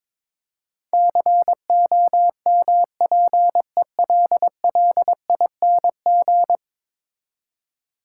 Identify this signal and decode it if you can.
Morse code, heard as a keyed tone, about 22 wpm, 700 Hz. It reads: COMPELLING
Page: morse-code-ninja-practice-22wpm